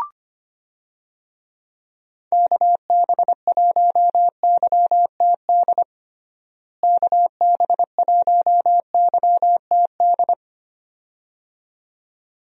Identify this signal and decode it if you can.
Morse code, heard as a keyed tone, about 25 wpm, 700 Hz. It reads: KB1YTD KB1YTD
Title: USA Call Signs - 6 Characters - Repeated 2x - 25wpm